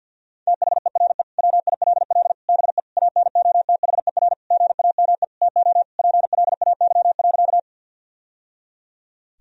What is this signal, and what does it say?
Morse code, heard as a keyed tone, about 52 wpm, 700 Hz. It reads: THERE WILL BE ANOTHER GAME TO PLAY.
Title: Sentences from Top 1000 Words - Review Part 3 - 50wpm